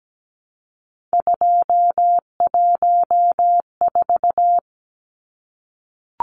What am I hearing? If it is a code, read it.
Morse code, heard as a keyed tone, about 17 wpm, 700 Hz. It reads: 214
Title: Numbers - 3-Digit 17wpm